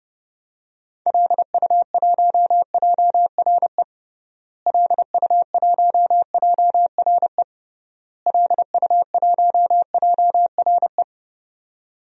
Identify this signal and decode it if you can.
Morse code, heard as a keyed tone, about 32 wpm, 700 Hz. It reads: LU1JRE LU1JRE LU1JRE
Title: DXCC-100 Call Signs - 6 Characters - Repeated 3x - 30wpm